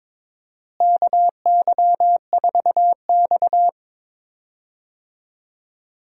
Morse code, heard as a keyed tone, about 22 wpm, 700 Hz. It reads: KY4X
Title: USA Call Signs - 4 Characters - Repeated 3x - 22wpm